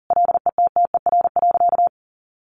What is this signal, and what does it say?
Morse code, heard as a keyed tone, about 40 wpm, 700 Hz. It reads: LETTER.
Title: Sentences from Top 1500 Words 40wpm